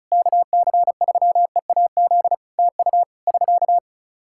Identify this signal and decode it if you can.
Morse code, heard as a keyed tone, about 36 wpm, 700 Hz. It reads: KC3EAZ TU <SK>
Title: Practice QSO - Exchange 4 - Ending - 35wpm